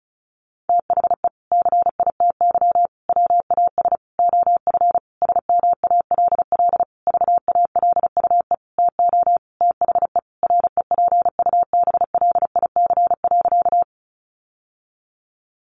Morse code, heard as a keyed tone, about 34 wpm, 700 Hz. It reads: THE CITY WAS OF SMALL VALUE TO THE REPUBLIC.
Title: Sentences from Top 1500 Words 35wpm